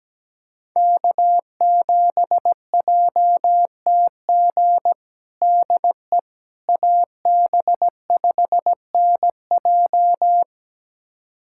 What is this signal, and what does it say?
Morse code, heard as a keyed tone, about 17 wpm, 700 Hz. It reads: K7JTG DE AB5NJ
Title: USA Callsign DE Callsign - 5 Characters - 17wpm